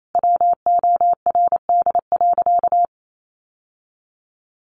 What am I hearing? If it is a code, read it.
Morse code, heard as a keyed tone, about 28 wpm, 700 Hz. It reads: WORD.
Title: Sentences from Top 1100 Words 28wpm